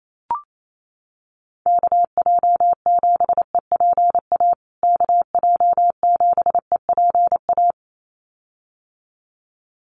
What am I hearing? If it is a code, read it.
Morse code, heard as a keyed tone, about 28 wpm, 700 Hz. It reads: KJ7EPA KJ7EPA
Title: USA Call Signs Repeated 2x 28wpm